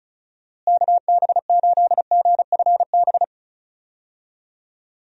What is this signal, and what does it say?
Morse code, heard as a keyed tone, about 34 wpm, 700 Hz. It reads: KB8GFB
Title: USA Call Signs - 6 Characters - Repeated 2x - 35wpm